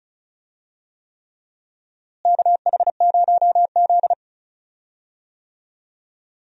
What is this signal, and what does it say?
Morse code, heard as a keyed tone, about 36 wpm, 700 Hz. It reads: KH0Z
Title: USA Call Signs - 4 Characters - Repeated 3x - 35wpm